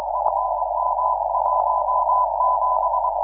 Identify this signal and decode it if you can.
Morse code, heard as a keyed tone, about 22 wpm, 1000 Hz, with noise. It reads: FM17
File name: iw0ffk / 2006 06 16 W4DR IW0FFK